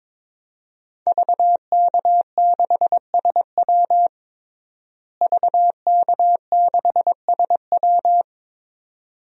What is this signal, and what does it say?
Morse code, heard as a keyed tone, about 22 wpm, 700 Hz. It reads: VK6SW VK6SW
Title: DXCC-100 Call Signs - 5 Characters - Repeated 2x - 22wpm